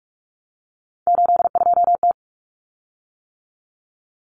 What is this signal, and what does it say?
Morse code, heard as a keyed tone, about 44 wpm, 700 Hz. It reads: 82T
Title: Common Cut Numbers - 3-Digit 45wpm